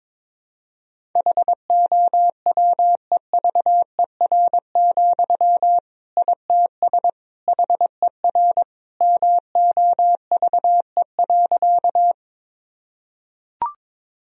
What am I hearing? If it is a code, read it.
Morse code, heard as a keyed tone, about 22 wpm, 700 Hz. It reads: HOWEVER, ITS HER MOVE.
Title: Sentences from Top 1000 Words - Review Part 1 - 22wpm